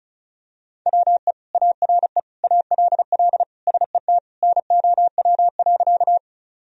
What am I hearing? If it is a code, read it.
Morse code, heard as a keyed tone, about 36 wpm, 700 Hz. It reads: WE ARE ALL SET NOW.
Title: Sentences from Top 1000 Words - Review Part 1 - 35wpm